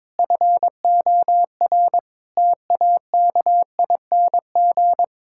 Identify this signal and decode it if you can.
Morse code, heard as a keyed tone, about 22 wpm, 700 Hz. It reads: FOR TAKING
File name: Sentences with 7-Word Phrases - 22wpm